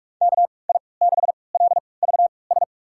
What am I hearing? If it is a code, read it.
Morse code, heard as a keyed tone, about 45 wpm, spaced as for 25 wpm, 700 Hz. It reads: KI6LVS
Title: USA Call Signs - 6 Characters - 3x Character Spacing - 45wpm